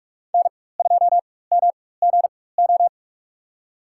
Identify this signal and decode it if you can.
Morse code, heard as a keyed tone, about 46 wpm, spaced as for 21 wpm, 700 Hz. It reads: N2MGO
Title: USA Call Signs - 5 Characters - 4x Character Spacing - 45wpm